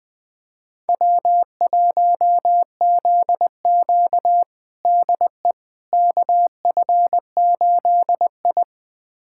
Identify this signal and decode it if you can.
Morse code, heard as a keyed tone, about 20 wpm, 700 Hz. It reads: W1ZQ DE KF8I
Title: USA Callsign DE Callsign - 4 Characters - 20wpm